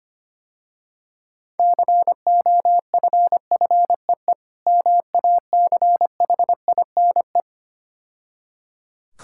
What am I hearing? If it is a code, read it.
Morse code, heard as a keyed tone, about 25 wpm, 700 Hz. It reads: COFFEE MACHINE
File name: practice / 25wpm / Difficult Words - Speed-Racing with Hints - 25wpm